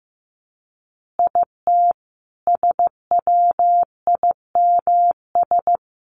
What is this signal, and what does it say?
Morse code, heard as a keyed tone, about 14 wpm, 700 Hz. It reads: IT SWIMS
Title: Additional Easy Phrases 15wpm